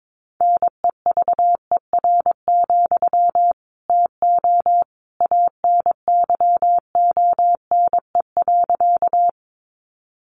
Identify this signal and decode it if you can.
Morse code, heard as a keyed tone, about 22 wpm, 700 Hz. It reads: NEVER, TO ANYONE.
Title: Sentences from Top 900 Words 22wpm